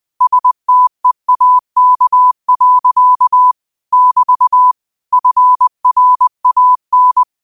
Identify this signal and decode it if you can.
Morse code, heard as a keyed tone, about 20 wpm, 1000 Hz. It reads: STEAK. <BT> FRAN
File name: 20wpm